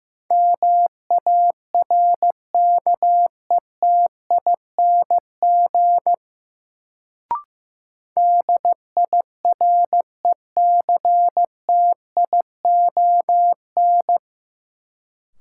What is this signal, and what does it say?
Morse code, heard as a keyed tone, about 15 wpm, 700 Hz. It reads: MARKETING DIRECTION
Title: Sentences from Top 1000 Words - Review Part 5 - New Word Drill - 15wpm